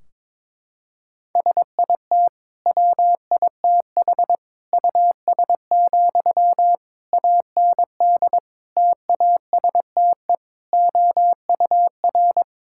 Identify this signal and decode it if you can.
Morse code, heard as a keyed tone, about 21 wpm, 700 Hz. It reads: SIT WITH US, AND TASTE OUR
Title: Sentences from Top 1600 Words 22wpm